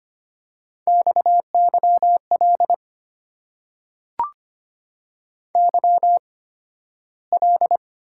Morse code, heard as a keyed tone, about 25 wpm, spaced as for 13 wpm, 700 Hz. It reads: XYL YL
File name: Top 90 QSO Elements - 8x Character Spacing - 25wpm